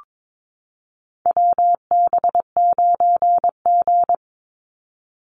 Morse code, heard as a keyed tone, about 22 wpm, 700 Hz. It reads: WB9G
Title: Call Signs - 22wpm